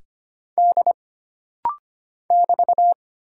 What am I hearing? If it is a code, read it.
Morse code, heard as a keyed tone, about 25 wpm, 700 Hz. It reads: D <BT>
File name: Single Character (Letter:Number:Punctuation:Prosign) - Rapid-Fire - 25wpm